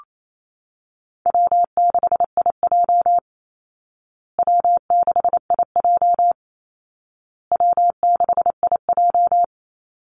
Morse code, heard as a keyed tone, about 28 wpm, 700 Hz. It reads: W6IJ W6IJ W6IJ
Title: Call Signs Repeated 2x - 28wpm